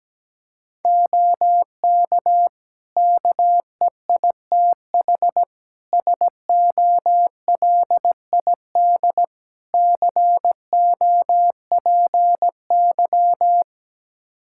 Morse code, heard as a keyed tone, about 17 wpm, 700 Hz. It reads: OK KEITH SOLID COPY
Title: Practice QSO - Exchange 3 - More Info - 17wpm